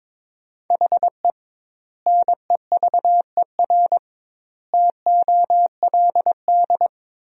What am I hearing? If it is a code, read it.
Morse code, heard as a keyed tone, about 22 wpm, 700 Hz. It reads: HE NEVER TOLD
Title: 2x Word-Spacing - Sentences from Top 400 Words - 22wpm